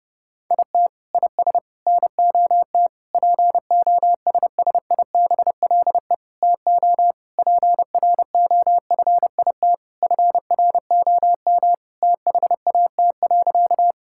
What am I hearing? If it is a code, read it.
Morse code, heard as a keyed tone, about 32 wpm, 700 Hz. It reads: IT IS NOT POSSIBLE TO PROFIT FROM THAT.
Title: Sentences from Top 1000 Words - Review Part 5 - 30wpm